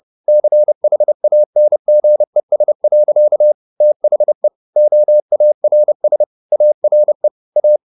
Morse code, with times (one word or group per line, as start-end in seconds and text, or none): 0.3-3.5 CHANGES.
3.8-4.5 THE
4.8-6.2 OARS
6.5-7.3 ARE
7.6-7.8 A